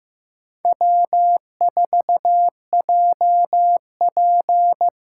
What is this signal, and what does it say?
Morse code, heard as a keyed tone, about 16 wpm, 700 Hz. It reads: W4JP
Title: USA Callsign DE Callsign - 4 Characters - 15wpm